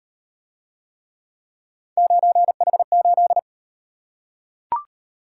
Morse code, heard as a keyed tone, about 38 wpm, 700 Hz. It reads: 9H8